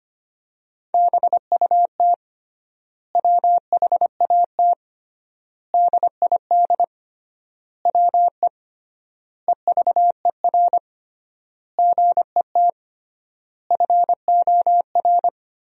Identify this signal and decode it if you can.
Morse code, heard as a keyed tone, about 25 wpm, 700 Hz. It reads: BUT WHAT DID WE EVER GET FOR
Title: 3x Word-Spacing - Sentences from Top 400 Words - 25wpm